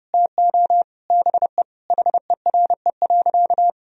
Morse code, heard as a keyed tone, about 32 wpm, 700 Hz. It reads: TO BE HERE.
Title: Sentences from Top 1000 Words - Review Part 1 - 30wpm